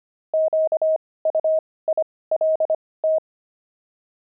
Morse code, heard as a keyed tone, about 25 wpm, spaced as for 18 wpm, 630 Hz. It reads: QUILT
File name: pangram-fs-18-cs-25-fragment